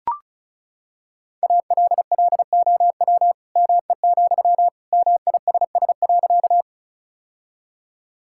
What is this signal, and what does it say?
Morse code, heard as a keyed tone, about 36 wpm, 700 Hz. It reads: ALLOW ME, MISS.
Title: Sentences from Top 1000 Words - Review Part 5 - 35wpm